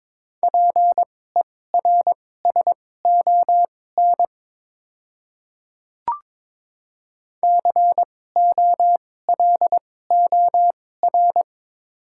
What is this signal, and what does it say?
Morse code, heard as a keyed tone, about 22 wpm, spaced as for 16 wpm, 700 Hz. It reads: PERSON COLOR
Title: 2x Character Spacing - Top 500 Words - 22wpm